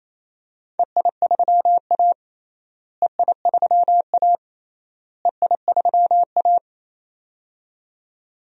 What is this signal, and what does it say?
Morse code, heard as a keyed tone, about 28 wpm, 700 Hz. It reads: EI3A EI3A EI3A
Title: DXCC-100 Call Signs - 4 Characters - Repeated 3x - 28wpm